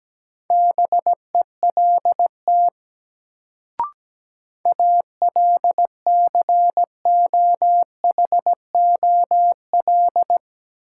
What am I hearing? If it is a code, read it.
Morse code, heard as a keyed tone, about 17 wpm, 700 Hz. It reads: BELT ALCOHOL
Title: Sentences from Top 1400 Words - New Word Drill Alt - 17wpm